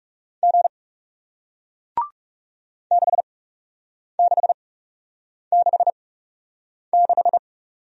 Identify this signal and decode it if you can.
Morse code, heard as a keyed tone, about 41 wpm, spaced as for 8 wpm, 700 Hz. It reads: G 6666